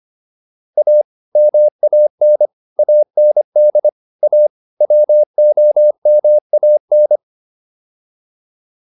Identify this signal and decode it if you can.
Morse code, heard as a keyed tone, about 25 wpm, 600 Hz. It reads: A MAN AND A WOMAN